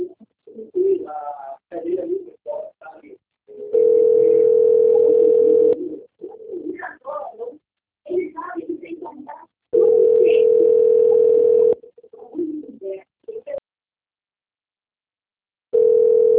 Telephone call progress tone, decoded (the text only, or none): ringback tone